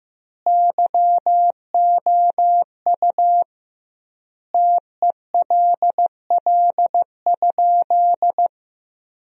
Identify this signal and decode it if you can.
Morse code, heard as a keyed tone, about 15 wpm, 700 Hz. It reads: YOU TELL?
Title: 2x Word-Spacing - Sentences from Top 200 Words - 15wpm